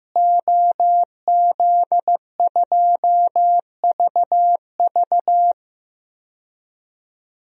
Morse code, heard as a keyed tone, about 15 wpm, 700 Hz. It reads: OZ2VV
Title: DXCC-100 Call Signs - 5 Characters - Repeated 2x - 15wpm